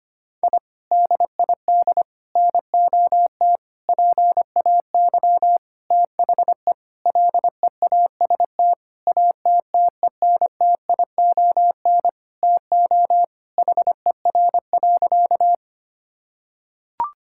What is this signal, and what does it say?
Morse code, heard as a keyed tone, about 25 wpm, 700 Hz. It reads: I DID NOT PAY THE LEAST ATTENTION TO HER.
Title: Sentences from Top 1000 Words - Review Part 6 - 25wpm